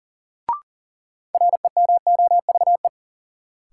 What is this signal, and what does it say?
Morse code, heard as a keyed tone, about 39 wpm, 700 Hz. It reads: REMOVE
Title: Sentences from Top 800 Words - New Word Drill - 40wpm